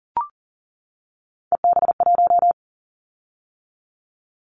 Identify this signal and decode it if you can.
Morse code, heard as a keyed tone, about 38 wpm, 700 Hz. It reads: EB1
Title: Callsign Prefixes for Top 25 DXCC - 40wpm